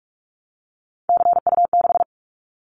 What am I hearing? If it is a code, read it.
Morse code, heard as a keyed tone, about 44 wpm, 700 Hz. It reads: CU6
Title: Callsign Prefixes for Top 100 DXCC - 45wpm